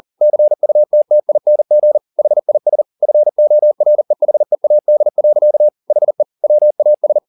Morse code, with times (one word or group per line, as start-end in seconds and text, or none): 0.2-2.0 CUTTING
2.2-2.8 HIS
3.0-5.7 FOREHEAD.
5.9-6.2 HE
6.4-7.2 WAS